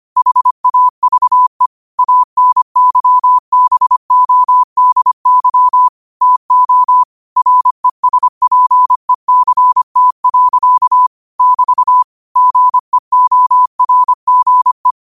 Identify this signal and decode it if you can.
Morse code, heard as a keyed tone, about 25 wpm, 1000 Hz. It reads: HAVE ANYBODY TO RESPECT. <BT> GEORGE
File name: qotdfiles / 25wpm